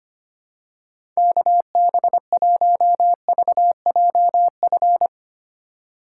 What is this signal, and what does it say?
Morse code, heard as a keyed tone, about 25 wpm, 700 Hz. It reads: KB1VJF
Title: USA Call Signs - 6 Characters - 4x Character Spacing - 25wpm